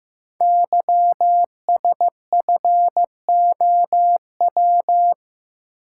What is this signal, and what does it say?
Morse code, heard as a keyed tone, about 15 wpm, 700 Hz. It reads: YSFOW
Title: Code Groups - 5 Letter:Numbers - 15wpm